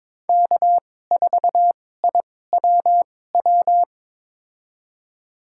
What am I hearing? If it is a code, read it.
Morse code, heard as a keyed tone, about 22 wpm, spaced as for 16 wpm, 700 Hz. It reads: K4IWW
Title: USA Call Signs - 5 Characters - 2x Character Spacing - 22wpm